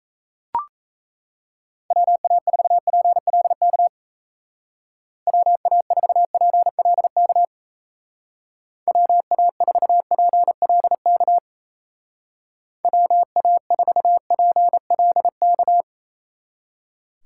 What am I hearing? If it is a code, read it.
Morse code, heard as a keyed tone, about 34 wpm, 700 Hz. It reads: WA4PLK WA4PLK WA4PLK WA4PLK
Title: Speed Racing - USA 2x3 Call Signs 28wpm